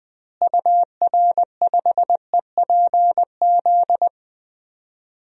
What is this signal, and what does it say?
Morse code, heard as a keyed tone, about 20 wpm, 700 Hz. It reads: UR5EPZ
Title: DXCC-100 Call Signs - 6 Characters - Repeated 2x - 20wpm